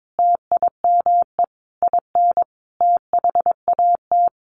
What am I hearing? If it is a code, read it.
Morse code, heard as a keyed tone, about 22 wpm, 700 Hz. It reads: TIME IN THAT
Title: Sentences from Top 900 Words 22wpm